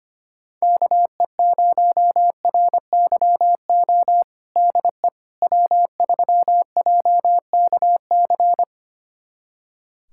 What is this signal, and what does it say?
Morse code, heard as a keyed tone, about 25 wpm, 700 Hz. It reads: KE0RYO DE W3JKC
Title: USA Callsign DE Callsign 25wpm